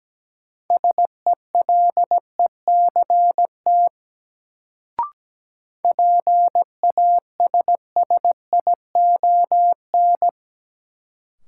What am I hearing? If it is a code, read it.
Morse code, heard as a keyed tone, about 17 wpm, 700 Hz. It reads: SELECT PASSION
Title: Sentences from Top 1300 Words - New Word Drill - 17wpm